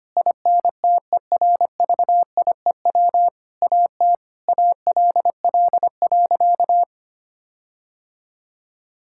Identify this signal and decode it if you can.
Morse code, heard as a keyed tone, about 25 wpm, 700 Hz. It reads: INTERVIEW AT ALL.
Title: Sentences from Top 1600 Words 25wpm